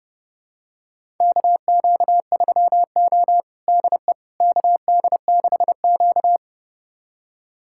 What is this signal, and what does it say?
Morse code, heard as a keyed tone, about 30 wpm, 700 Hz. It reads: KQ3O DE KD6Q
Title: USA Callsign DE Callsign - 4 Characters - 30wpm